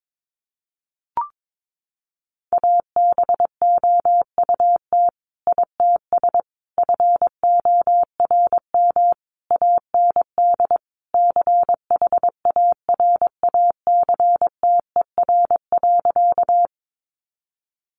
Morse code, heard as a keyed tone, about 22 wpm, 700 Hz. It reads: ABOUT ITS FORM AND CHARACTER.
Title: Sentences from Top 1200 Words 22wpm